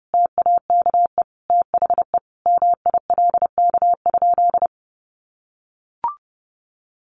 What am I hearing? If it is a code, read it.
Morse code, heard as a keyed tone, about 30 wpm, 700 Hz. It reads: TAKE THE MILK?
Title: Sentences from Top 1500 Words 30wpm